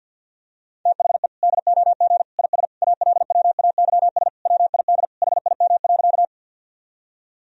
Morse code, heard as a keyed tone, about 51 wpm, 700 Hz. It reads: THE DOG IS ALWAYS WID HIM.
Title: Sentences from Top 400 Words and Abbreviations 50wpm